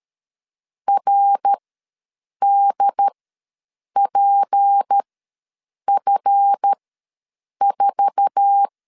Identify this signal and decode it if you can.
Morse code, heard as a keyed tone, about 13 wpm, spaced as for 7 wpm, 800 Hz. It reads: RDPF4
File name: LEZIONI / 8 YJ45 / 8 YJ4 ws 12wpm 3249